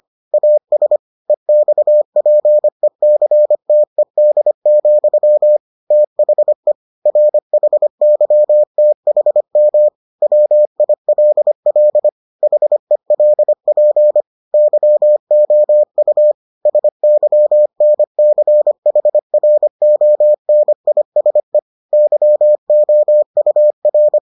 Morse code, heard as a keyed tone, about 25 wpm, 600 Hz. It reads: AS EXPECTED, THE RHYTHM WILL HELP YOU SYNCHRONISE YOUR